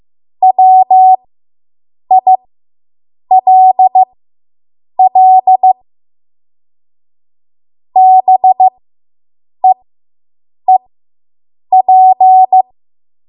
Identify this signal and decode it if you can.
Morse code, heard as a keyed tone, about 15 wpm, spaced as for 7 wpm, 750 Hz. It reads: WILL BEEP